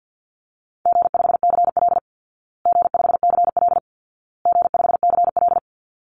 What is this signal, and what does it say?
Morse code, heard as a keyed tone, about 50 wpm, 700 Hz. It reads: G5CL G5CL G5CL
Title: Easy Call Signs Repeated 3x - 50wpm